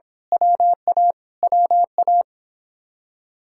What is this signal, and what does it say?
Morse code, heard as a keyed tone, about 24 wpm, 700 Hz. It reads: WA WA
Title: Speed-Racing Common QSO Phrases - 22wpm